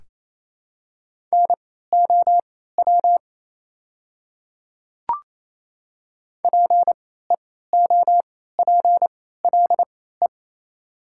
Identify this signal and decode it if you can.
Morse code, heard as a keyed tone, about 28 wpm, spaced as for 16 wpm, 700 Hz. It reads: NOW PEOPLE
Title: 3x Character Spacing - Top 300 Words - 28wpm